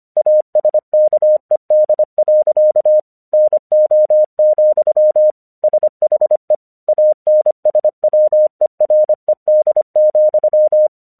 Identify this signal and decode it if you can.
Morse code, heard as a keyed tone, about 25 wpm, 610 Hz. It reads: ASKED. NO, SHE ANSWERED,
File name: 1 English 25wpm / AQC03